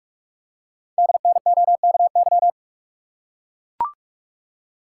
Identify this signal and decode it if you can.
Morse code, heard as a keyed tone, about 44 wpm, 700 Hz. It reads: DNOKY
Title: Code Groups - 5 Letters - 45wpm